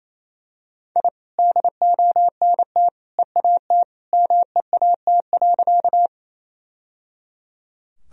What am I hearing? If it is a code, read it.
Morse code, heard as a keyed tone, about 28 wpm, 700 Hz. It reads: I DONT EAT MEAT.
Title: Sentences from Top 1000 Words - Review Part 3 - 28wpm